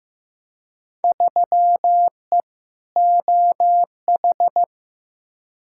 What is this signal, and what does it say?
Morse code, heard as a keyed tone, about 15 wpm, 700 Hz. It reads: 3E OH
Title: Field Day Exchanges - 15wpm